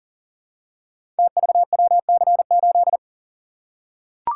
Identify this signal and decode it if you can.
Morse code, heard as a keyed tone, about 38 wpm, 700 Hz. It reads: TVWC8
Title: Code Groups - 5 Letter:Numbers - 40wpm